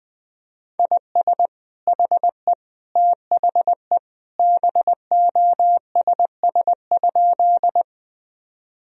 Morse code, heard as a keyed tone, about 20 wpm, 700 Hz. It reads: IS HE THE BOSS?
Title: Sentences from Top 1000 Words - Review Part 4 - 20wpm